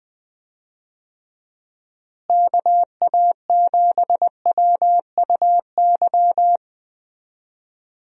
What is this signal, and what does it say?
Morse code, heard as a keyed tone, about 20 wpm, 700 Hz. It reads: KA7WUY
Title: USA Call Signs - 6 Characters - Repeated 2x - 20wpm